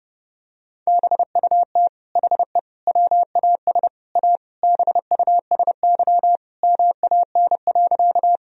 Morse code, heard as a keyed tone, about 30 wpm, 700 Hz. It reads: BUT HE WAS A BUSY MAN.
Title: Sentences from Top 1000 Words - Review Part 3 - 30wpm